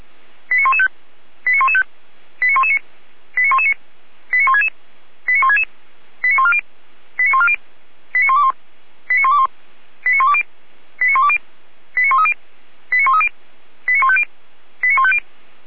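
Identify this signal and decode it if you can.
Five-tone selective call (ZVEI), heard as quick runs of five tones, tones of about 70 ms. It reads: 89115 89115 89119 89119 89166 89166 89140 89140 89121 89121 89120 89120 89130 89130 89150 89150
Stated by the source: ZVEI five-tone call received by SDR, recorded by Feuerwehr Orlamünde.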